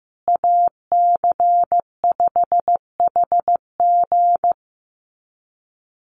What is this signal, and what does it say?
Morse code, heard as a keyed tone, about 15 wpm, 700 Hz. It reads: AC5HG
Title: USA Call Signs Repeated 2x 15wpm